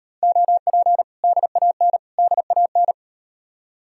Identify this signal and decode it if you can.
Morse code, heard as a keyed tone, about 38 wpm, 700 Hz. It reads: OP DAN DAN